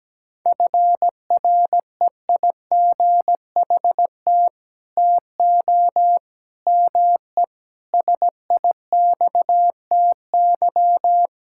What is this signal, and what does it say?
Morse code, heard as a keyed tone, about 17 wpm, 700 Hz. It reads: FREIGHT TO ME SIXTY
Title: Pangram Challenge v2 - 17wpm